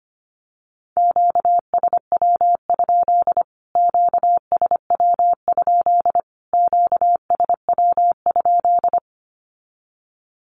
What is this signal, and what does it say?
Morse code, heard as a keyed tone, about 25 wpm, 700 Hz. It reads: QSW? QSW? QSW?
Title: Q-Codes Repeated 3x 25wpm